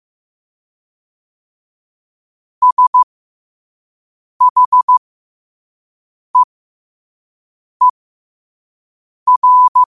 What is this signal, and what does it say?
Morse code, heard as a keyed tone, about 15 wpm, spaced as for 5 wpm, 1000 Hz. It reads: SHEER